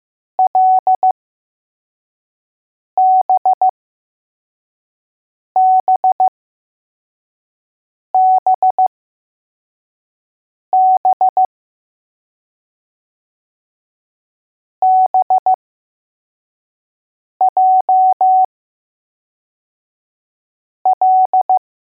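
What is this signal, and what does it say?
Morse code, heard as a keyed tone, about 15 wpm, spaced as for 4 wpm, 750 Hz. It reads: LBBBB BJL
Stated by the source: Morse code letters test drill for L, J and B.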